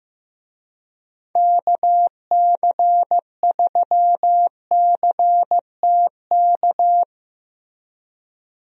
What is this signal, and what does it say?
Morse code, heard as a keyed tone, about 14 wpm, 700 Hz. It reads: KC3CTK
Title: USA Call Signs - 6 Characters - Repeated 3x - 15wpm